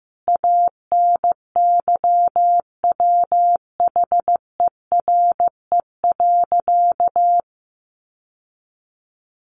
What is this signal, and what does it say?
Morse code, heard as a keyed tone, about 15 wpm, 700 Hz. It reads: ANYWHERE.
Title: Sentences from Top 1400 Words 15wpm